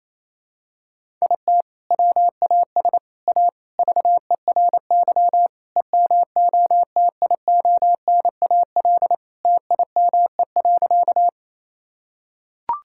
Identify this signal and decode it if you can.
Morse code, heard as a keyed tone, about 27 wpm, 700 Hz. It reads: IT WAS A VERY EMOTIONAL TIME.
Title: Sentences from Top 1000 Words - Review Part 5 - 28wpm